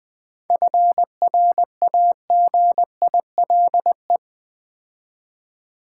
Morse code, heard as a keyed tone, about 20 wpm, 700 Hz. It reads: FRAGILE
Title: 7 Letter Words - First 3 Letters Given - 20wpm